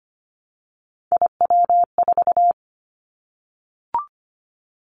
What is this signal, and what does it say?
Morse code, heard as a keyed tone, about 25 wpm, 700 Hz. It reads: IW4
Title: Callsign Prefixes for Top 25 DXCC - 25wpm